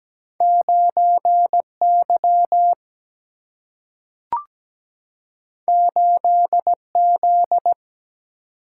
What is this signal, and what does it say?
Morse code, heard as a keyed tone, about 17 wpm, 700 Hz. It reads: 9Y 8Z
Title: Single-Letter Single-Number 17wpm